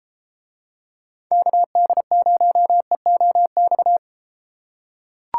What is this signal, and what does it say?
Morse code, heard as a keyed tone, about 31 wpm, 700 Hz. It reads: KD0EOX